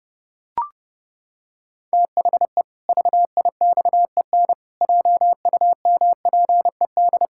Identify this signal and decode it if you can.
Morse code, heard as a keyed tone, about 32 wpm, 700 Hz. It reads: THE VIXEN JUMPED
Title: Pangram Challenge v2 - 30wpm